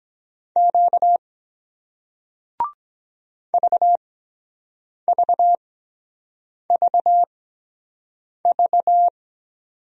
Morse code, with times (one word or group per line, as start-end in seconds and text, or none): 0.6-1.2 Q
3.5-9.1 VVVV